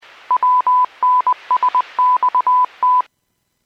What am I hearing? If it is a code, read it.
Morse code, heard as a keyed tone, about 20 wpm, 1000 Hz. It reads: WNSXT